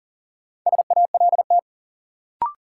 Morse code, heard as a keyed tone, about 40 wpm, 700 Hz. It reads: SALT
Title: Sentences from Top 800 Words - New Word Drill - 40wpm